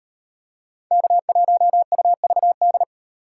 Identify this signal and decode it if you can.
Morse code, heard as a keyed tone, about 38 wpm, 700 Hz. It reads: K1UVD